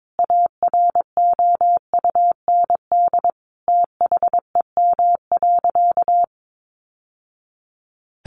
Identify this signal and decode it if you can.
Morse code, heard as a keyed tone, about 22 wpm, 700 Hz. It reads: AROUND THEM.
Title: Sentences from Top 600 Words 22wpm